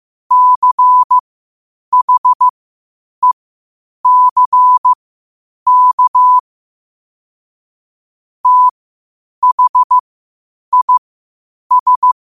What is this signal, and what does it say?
Morse code, heard as a keyed tone, about 15 wpm, spaced as for 8 wpm, 1000 Hz. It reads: CHECK THIS
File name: qotdfiles / 8wpm